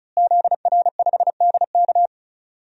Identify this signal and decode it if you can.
Morse code, heard as a keyed tone, about 36 wpm, 700 Hz. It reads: ZR5DK
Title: DXCC-100 Call Signs - 5 Characters - Repeated 2x - 35wpm